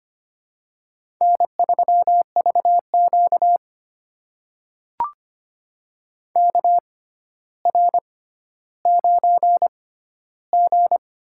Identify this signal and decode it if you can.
Morse code, heard as a keyed tone, about 25 wpm, spaced as for 13 wpm, 700 Hz. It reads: N3VQ KR9G
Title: USA Call Signs - 4 Characters - 6x Character Spacing - 25wpm